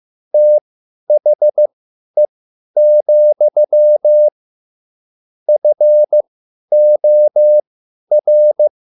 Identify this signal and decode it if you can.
Morse code, heard as a keyed tone, about 15 wpm, spaced as for 10 wpm, 600 Hz. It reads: THE, FOR
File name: QSL / CW lesson